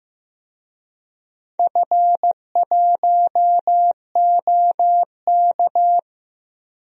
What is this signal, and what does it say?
Morse code, heard as a keyed tone, about 15 wpm, 700 Hz. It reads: F1OK
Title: DXCC-100 Call Signs - 4 Characters - Repeated 3x - 15wpm